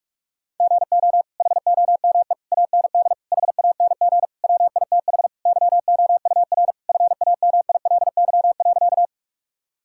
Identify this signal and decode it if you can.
Morse code, heard as a keyed tone, about 41 wpm, 700 Hz. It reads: GO HOME AND HANG WITH YOUR FAMILY.
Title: Sentences from Top 1600 Words 45wpm